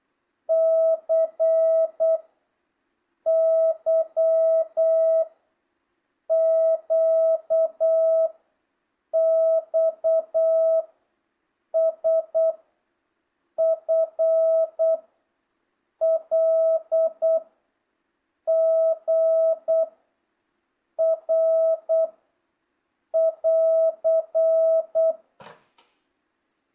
Morse code, heard as a keyed tone, about 8 wpm, 650 Hz. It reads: C Y Q X S F L G R <AR>